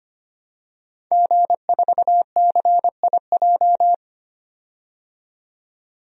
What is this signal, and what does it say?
Morse code, heard as a keyed tone, about 25 wpm, 700 Hz. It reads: G4CIJ
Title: DXCC-100 Call Signs - 5 Characters - 25wpm